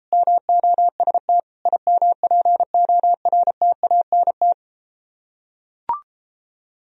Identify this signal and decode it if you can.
Morse code, heard as a keyed tone, about 33 wpm, 700 Hz. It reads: MOST IMPORTANT